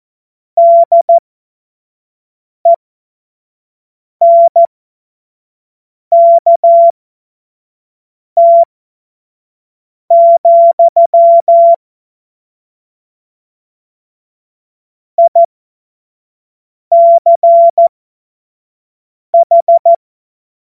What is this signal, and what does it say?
Morse code, heard as a keyed tone, about 14 wpm, spaced as for 5 wpm, 680 Hz. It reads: DENKT, ICH
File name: Morsekurs / Witze / Witz 27 Februar 2024 25 F70